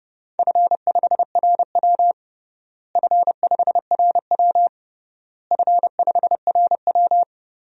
Morse code, heard as a keyed tone, about 32 wpm, 700 Hz. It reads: F5RW F5RW F5RW
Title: DXCC-100 Call Signs - 4 Characters - Repeated 3x - 30wpm